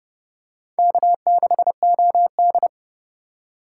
Morse code, heard as a keyed tone, about 30 wpm, 700 Hz. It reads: K6OD
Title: USA Call Signs - 4 Characters - 6x Character Spacing - 30wpm